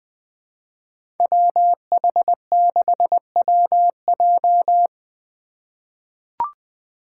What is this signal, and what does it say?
Morse code, heard as a keyed tone, about 20 wpm, 700 Hz. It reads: WH6WJ
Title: USA Call Signs - 2x Character Spacing - 20wpm